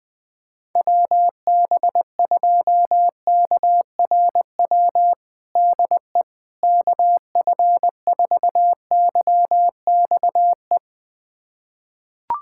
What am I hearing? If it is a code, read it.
Morse code, heard as a keyed tone, about 20 wpm, 700 Hz. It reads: WB2KRW DE KF4YXE
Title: USA Callsign DE Callsign - 6 Characters - 20wpm